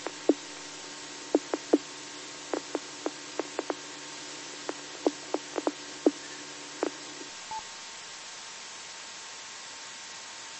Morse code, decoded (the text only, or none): none